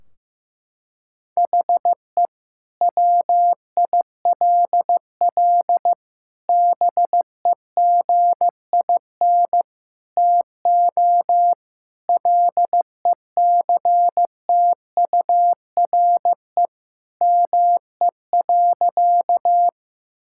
Morse code, heard as a keyed tone, about 15 wpm, 700 Hz. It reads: HE WILL BEGIN TO LECTURE ME.
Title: Sentences from Top 1600 Words 15wpm